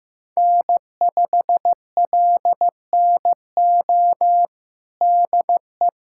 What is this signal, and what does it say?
Morse code, heard as a keyed tone, about 15 wpm, 700 Hz. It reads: N5LNO DE
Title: USA Callsign DE Callsign - 5 Characters - 15wpm